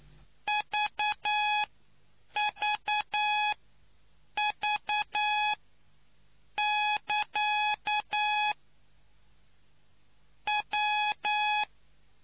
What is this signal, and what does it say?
Morse code, heard as a keyed tone, about 9 wpm, spaced as for 6 wpm, 840 Hz. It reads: VVV<KA> W